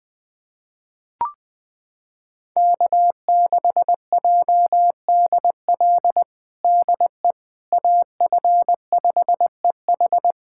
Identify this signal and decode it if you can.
Morse code, heard as a keyed tone, about 20 wpm, 700 Hz. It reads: K6JDL DE AF5EH
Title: USA Callsign DE Callsign - 5 Characters - 20wpm